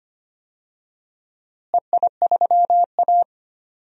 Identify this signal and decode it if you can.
Morse code, heard as a keyed tone, about 25 wpm, 700 Hz. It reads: EI3A
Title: DXCC-100 Call Signs - 4 Characters - Repeated 3x - 25wpm